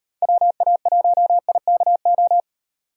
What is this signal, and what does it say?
Morse code, heard as a keyed tone, about 38 wpm, 700 Hz. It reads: WA1IKO